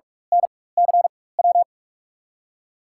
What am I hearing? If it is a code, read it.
Morse code, heard as a keyed tone, about 46 wpm, spaced as for 21 wpm, 700 Hz. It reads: NCW
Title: USA Call Signs - 6 Characters - 4x Character Spacing - 45wpm